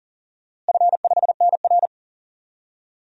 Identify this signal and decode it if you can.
Morse code, heard as a keyed tone, about 39 wpm, 700 Hz. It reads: F5NR
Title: DXCC-100 Call Signs - 4 Characters - Repeated 3x - 40wpm